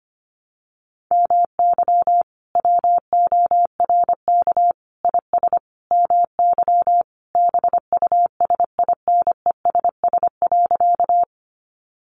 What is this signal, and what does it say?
Morse code, heard as a keyed tone, about 25 wpm, 700 Hz. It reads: MY WORK IS MY BUSINESS.
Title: Sentences from Top 600 Words 25wpm